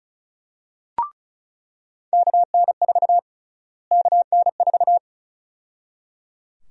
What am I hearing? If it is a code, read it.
Morse code, heard as a keyed tone, about 34 wpm, 700 Hz. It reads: KN4 KN4
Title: USA Call Sign Prefixes - Repeated 2x - 35wpm